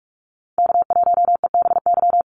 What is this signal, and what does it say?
Morse code, heard as a keyed tone, about 44 wpm, 700 Hz. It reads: K1EBY
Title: Call Signs - 45wpm